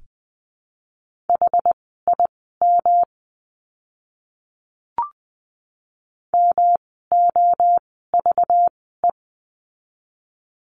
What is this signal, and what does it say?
Morse code, heard as a keyed tone, about 20 wpm, spaced as for 14 wpm, 700 Hz. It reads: HIM MOVE
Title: 237 Lesson 38f - Letter q - Words 20wpm